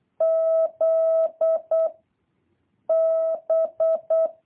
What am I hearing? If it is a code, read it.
Morse code, heard as a keyed tone, about 8 wpm, 650 Hz. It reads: Z B
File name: jh8cbh / 2023cw